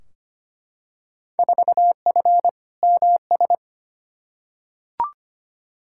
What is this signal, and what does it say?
Morse code, heard as a keyed tone, about 25 wpm, 700 Hz. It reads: 4F MS
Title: Field Day Exchanges Encore - 25wpm